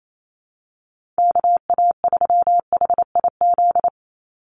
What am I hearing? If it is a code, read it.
Morse code, heard as a keyed tone, about 28 wpm, 700 Hz. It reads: KA3HIZ
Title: USA Call Signs 28wpm